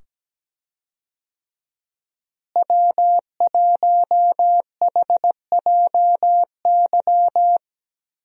Morse code, heard as a keyed tone, about 17 wpm, 700 Hz. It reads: W1HJY
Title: USA Call Signs - 5 Characters - Repeated 2x - 17wpm